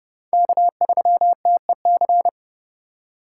Morse code, heard as a keyed tone, about 29 wpm, 700 Hz. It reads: K3TEC